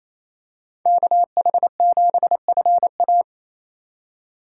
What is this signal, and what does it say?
Morse code, heard as a keyed tone, about 28 wpm, 700 Hz. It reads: KH7FA
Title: USA Call Signs - 5 Characters - 4x Character Spacing - 28wpm